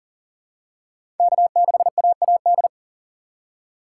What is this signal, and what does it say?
Morse code, heard as a keyed tone, about 40 wpm, 700 Hz. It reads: K6AAD
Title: USA Call Signs - 5 Characters - 3x Character Spacing - 40wpm